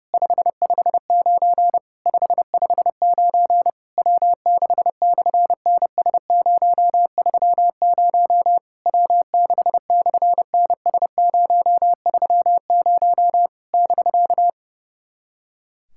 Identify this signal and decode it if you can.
Morse code, heard as a keyed tone, about 30 wpm, 700 Hz. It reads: 559 559 W6/NS030 W6/NS030 <BK>
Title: SOTA QSO Practice 30wpm